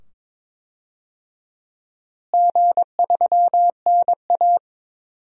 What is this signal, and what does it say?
Morse code, heard as a keyed tone, about 22 wpm, 700 Hz. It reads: G3NA
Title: DXCC-100 Call Signs - 4 Characters - Repeated 2x - 22wpm